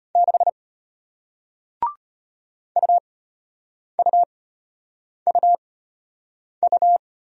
Morse code, heard as a keyed tone, about 33 wpm, spaced as for 8 wpm, 700 Hz. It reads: 6 UUUU